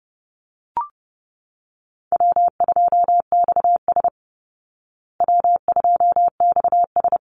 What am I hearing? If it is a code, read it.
Morse code, heard as a keyed tone, about 30 wpm, 700 Hz. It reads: W2XS W2XS
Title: Easy Call Signs Repeated 3x - 30wpm